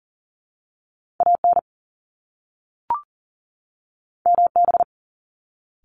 Morse code, heard as a keyed tone, about 36 wpm, 700 Hz. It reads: AN MB
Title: Top Two Letter Combinations in Words 40wpm